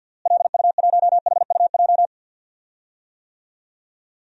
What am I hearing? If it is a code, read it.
Morse code, heard as a keyed tone, about 52 wpm, 700 Hz. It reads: LU1HUJ
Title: DXCC-100 Call Signs - 6 Characters - Repeated 3x - 50wpm